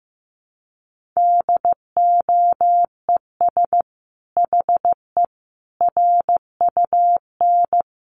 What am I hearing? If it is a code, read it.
Morse code, heard as a keyed tone, about 14 wpm, 700 Hz. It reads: DOES HE RUN
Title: Sentences from Top 1300 Words 15wpm